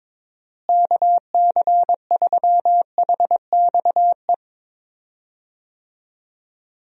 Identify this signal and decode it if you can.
Morse code, heard as a keyed tone, about 21 wpm, 700 Hz. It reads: KC3HXE
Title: USA Call Signs - 6 Characters - Repeated 3x - 22wpm